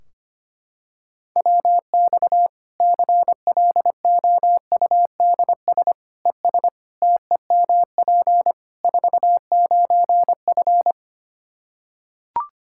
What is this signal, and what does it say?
Morse code, heard as a keyed tone, about 25 wpm, 700 Hz. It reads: WX CLOUDS ES TEMP 49F
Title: K1USN SST QSO Practice 25wpm